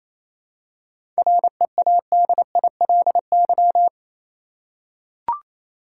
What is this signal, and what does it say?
Morse code, heard as a keyed tone, about 27 wpm, 700 Hz. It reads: READILY
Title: Seven Letter Words - 28wpm